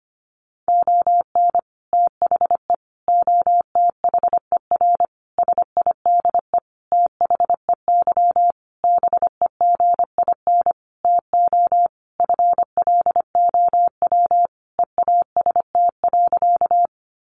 Morse code, heard as a keyed tone, about 25 wpm, 700 Hz. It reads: ON THE OTHER SIDE THEY BEGIN TO FLOW EAST.
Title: Sentences from Top 1200 Words 25wpm